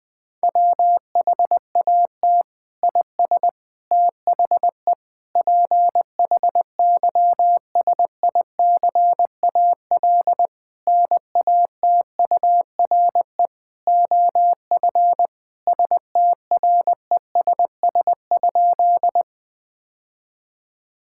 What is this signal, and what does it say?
Morse code, heard as a keyed tone, about 20 wpm, 700 Hz. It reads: WHAT IS THE PHYSICAL NATURE OF STRESS?
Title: Sentences from Top 1000 Words - Review Part 6 - 20wpm